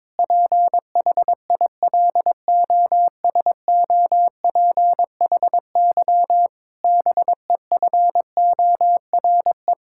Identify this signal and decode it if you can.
Morse code, heard as a keyed tone, about 22 wpm, 700 Hz. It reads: PHILOSOPHY BEFORE
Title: Sentences from Top 1000 Words - Review Part 6 - 22wpm